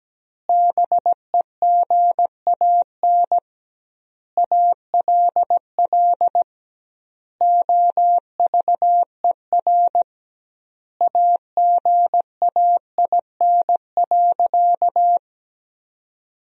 Morse code, heard as a keyed tone, about 17 wpm, 700 Hz. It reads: BEGAN ALL OVER AGAIN.
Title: 2x Word-Spacing - Sentences from Top 500 Words - 17wpm